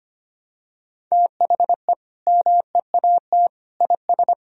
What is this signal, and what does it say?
Morse code, heard as a keyed tone, about 25 wpm, 700 Hz. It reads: THE MEAT IS
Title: Sentences from Top 1000 Words - Review Part 3 - 25wpm